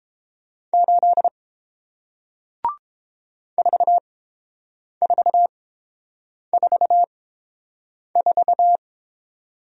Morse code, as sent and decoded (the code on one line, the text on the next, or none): ---.. / ....- ....- ....- ....-
8 4444